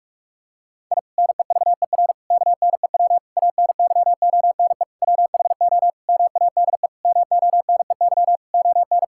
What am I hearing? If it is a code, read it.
Morse code, heard as a keyed tone, about 46 wpm, 700 Hz. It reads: I NEVER KNEW ANYONE WHO MADE MONEY ON
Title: Sentences from Top 1000 Words - Review Part 6 - 45wpm